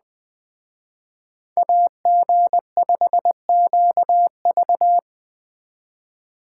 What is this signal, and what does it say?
Morse code, heard as a keyed tone, about 20 wpm, 700 Hz. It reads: AG5QV
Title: Speed Racing - USA Call Signs 17wpm